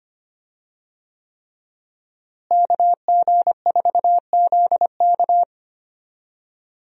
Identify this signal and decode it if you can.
Morse code, heard as a keyed tone, about 25 wpm, 700 Hz. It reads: KG4ZK
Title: USA Call Signs - 5 Characters - Repeated 3x - 25wpm